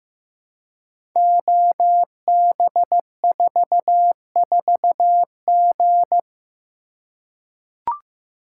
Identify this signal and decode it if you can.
Morse code, heard as a keyed tone, about 15 wpm, 700 Hz. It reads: OB44G
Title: Code Groups - 5 Letter:Numbers - 15wpm